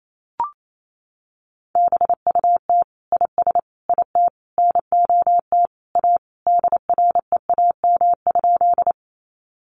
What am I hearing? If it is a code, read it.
Morse code, heard as a keyed tone, about 28 wpm, 700 Hz. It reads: BUT IS IT NOT A DREAM?
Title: Sentences from Top 1500 Words 28wpm